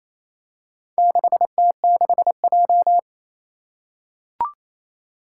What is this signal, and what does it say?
Morse code, heard as a keyed tone, about 28 wpm, 700 Hz. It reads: 6T6J
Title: Four Challenging Letter-Number Combinations 28wpm